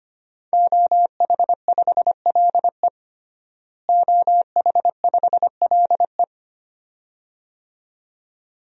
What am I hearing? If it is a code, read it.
Morse code, heard as a keyed tone, about 25 wpm, 700 Hz. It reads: OH5LE OH5LE
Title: DXCC-100 Call Signs - 5 Characters - Repeated 2x - 25wpm